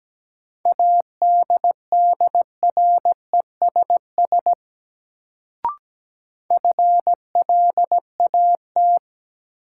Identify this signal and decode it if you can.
Morse code, heard as a keyed tone, about 17 wpm, 700 Hz. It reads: ADDRESS FLAT
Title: Sentences from Top 1000 Words - New Word Drill - 17wpm